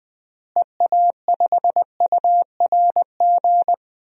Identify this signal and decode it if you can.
Morse code, heard as a keyed tone, about 20 wpm, 700 Hz. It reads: EA5URG
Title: DXCC-100 Call Signs - 6 Characters - Repeated 3x - 20wpm